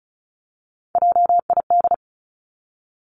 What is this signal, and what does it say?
Morse code, heard as a keyed tone, about 36 wpm, 700 Hz. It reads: JID
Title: Three Challenging Letter-Number Combinations 35wpm